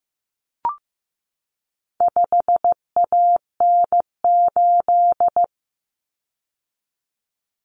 Morse code, heard as a keyed tone, about 15 wpm, 700 Hz. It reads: 5AN8
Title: Common Cut Numbers - 4-Digit 15wpm